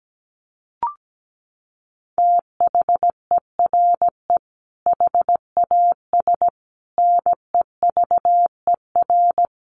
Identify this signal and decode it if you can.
Morse code, heard as a keyed tone, about 17 wpm, 700 Hz. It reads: THERE HAS NEVER
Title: Sentences from Top 1500 Words 17wpm